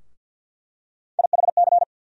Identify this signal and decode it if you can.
Morse code, heard as a keyed tone, about 48 wpm, 700 Hz. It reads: IHC
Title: USA Call Sign Postfixes - 3 Letters 50wpm